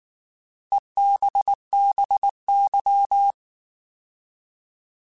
Morse code, heard as a keyed tone, about 19 wpm, 800 Hz. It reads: EBBY